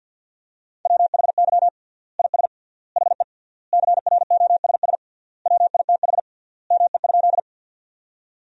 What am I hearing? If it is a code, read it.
Morse code, heard as a keyed tone, about 50 wpm, 700 Hz. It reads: WHY IS HE CROSS WITH ME?
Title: 3x Word-Spacing - Sentences from Top 300 Words - 50wpm